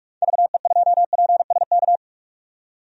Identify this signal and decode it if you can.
Morse code, heard as a keyed tone, about 43 wpm, 700 Hz. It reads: VE2PSK
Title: DXCC-100 Call Signs - 6 Characters - 45wpm